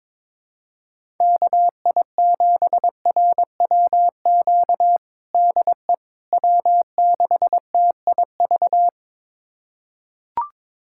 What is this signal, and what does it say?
Morse code, heard as a keyed tone, about 22 wpm, 700 Hz. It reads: KI7RWQ DE W6TIV
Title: USA Callsign DE Callsign 22wpm